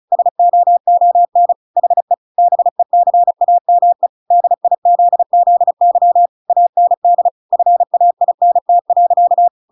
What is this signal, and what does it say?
Morse code, heard as a keyed tone, about 34 wpm, 700 Hz. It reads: SOON HE BECAME DIZZY AND FAINT.